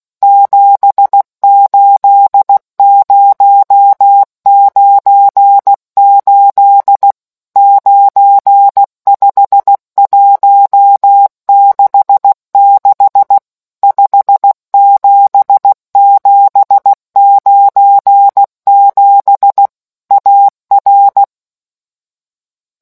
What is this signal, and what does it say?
Morse code, heard as a keyed tone, about 16 wpm, 800 Hz. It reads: 78098 95166 57797 AR